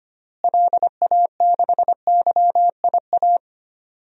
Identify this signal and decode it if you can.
Morse code, heard as a keyed tone, about 25 wpm, 700 Hz. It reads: LA6YIA
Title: DXCC-100 Call Signs - 6 Characters - Repeated 3x - 25wpm